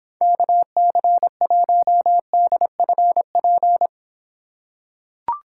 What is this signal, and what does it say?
Morse code, heard as a keyed tone, about 25 wpm, 700 Hz. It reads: KC1DFP